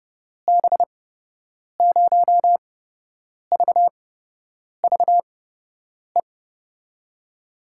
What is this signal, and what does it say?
Morse code, heard as a keyed tone, about 29 wpm, spaced as for 8 wpm, 700 Hz. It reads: B0VVE
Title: USA Call Signs - 6 Characters - 8x Character Spacing - 30wpm